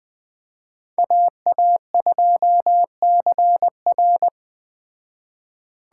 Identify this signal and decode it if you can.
Morse code, heard as a keyed tone, about 20 wpm, 700 Hz. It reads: AA2CR
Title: Speed Racing - USA 2x2 Call Signs 17wpm